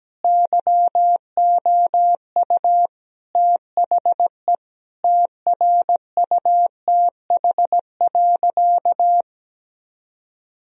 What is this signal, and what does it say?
Morse code, heard as a keyed tone, about 17 wpm, 700 Hz. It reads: YOU THE TRUTH.
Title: Sentences from Top 1000 Words - Review Part 2 - 17wpm